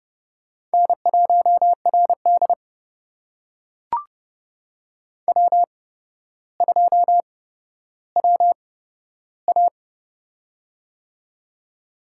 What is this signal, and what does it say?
Morse code, heard as a keyed tone, about 30 wpm, spaced as for 13 wpm, 700 Hz. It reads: N1RD W2WA
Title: USA Call Signs - 4 Characters - 8x Character Spacing - 30wpm